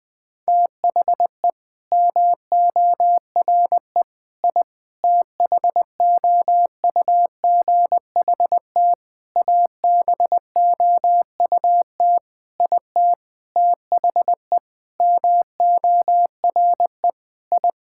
Morse code, heard as a keyed tone, about 19 wpm, 700 Hz. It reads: THE MORE I THOUGHT ABOUT IT THE MORE I
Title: Sentences with 9-Word Phrases - 20wpm